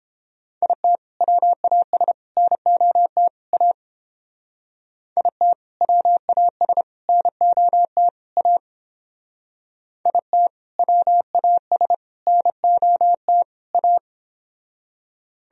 Morse code, heard as a keyed tone, about 30 wpm, 700 Hz. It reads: IT WAS NOT A IT WAS NOT A IT WAS NOT A